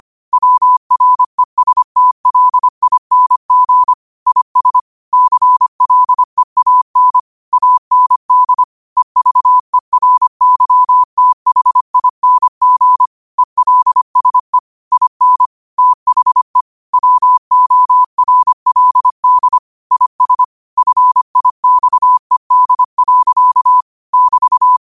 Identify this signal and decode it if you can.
Morse code, heard as a keyed tone, about 25 wpm, 1000 Hz. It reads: WRESTLING IS CLEAN AND EVERYTHING ELSE IN THE WORLD IS FIXED. <BT>